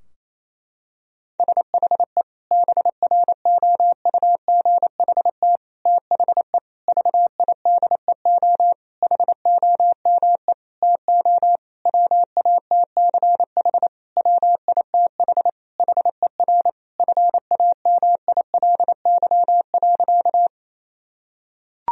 Morse code, heard as a keyed tone, about 27 wpm, 700 Hz. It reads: SHE BROUGHT THE VIDEO HOME TO WATCH WITH HER FAMILY.
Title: Sentences from Top 1000 Words - Review Part 4 - 28wpm